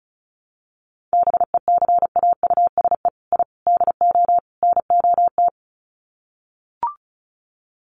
Morse code, heard as a keyed tone, about 34 wpm, 700 Hz. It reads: BECAUSE I DO NOT
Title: Sets of 4 Words 35wpm